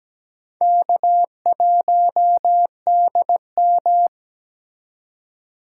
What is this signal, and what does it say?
Morse code, heard as a keyed tone, about 17 wpm, 700 Hz. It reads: K1DM
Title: K1USN SST QSO Practice - 17wpm